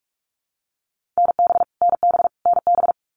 Morse code, heard as a keyed tone, about 44 wpm, 700 Hz. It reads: NB NB NB
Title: Canadian Provinces and Territories - Repeated 3x - 45wpm